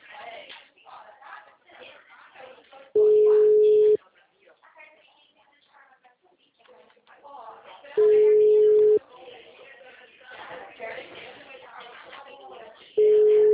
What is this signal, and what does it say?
Ringback tone (Europe, 425 Hz), with speech.
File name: out- -2002- - - .32771